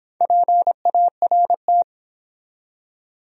Morse code, heard as a keyed tone, about 30 wpm, 700 Hz. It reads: PART